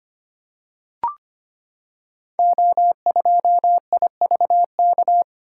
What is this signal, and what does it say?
Morse code, heard as a keyed tone, about 25 wpm, 700 Hz. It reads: O2IVK
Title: Code Groups - 5 Letter:Numbers - 25wpm